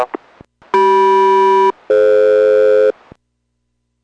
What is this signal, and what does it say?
Aircraft SELCAL, heard as two pairs of tones. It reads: CM-DF